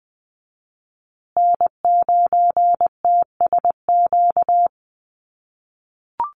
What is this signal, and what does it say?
Morse code, heard as a keyed tone, about 20 wpm, 700 Hz. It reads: N9TSQ
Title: USA Call Signs Repeated 2x 20wpm